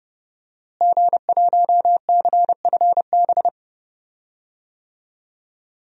Morse code, heard as a keyed tone, about 32 wpm, 700 Hz. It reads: G1CFB
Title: DXCC-100 Call Signs - 5 Characters - Repeated 2x - 30wpm